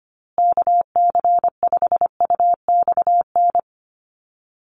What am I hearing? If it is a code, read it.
Morse code, heard as a keyed tone, about 25 wpm, 700 Hz. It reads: KC5UXN
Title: USA Call Signs Repeated 2x 25wpm